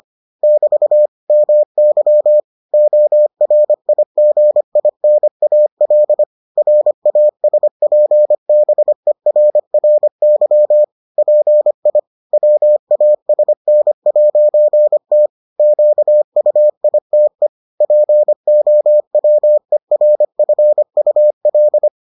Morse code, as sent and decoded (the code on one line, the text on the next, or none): -...- / -- -.-- / --- .-. .. --. .. -. .- .-.. / .-. .- ... .--. -... .-. .-. -.-- / .--. .. / .-- .- ... -. .----. - / --.- ..- .. - . / .--. --- .-- . .-. ..-. ..- .-..
<BT> MY ORIGINAL RASPBERRY PI WASN'T QUITE POWERFUL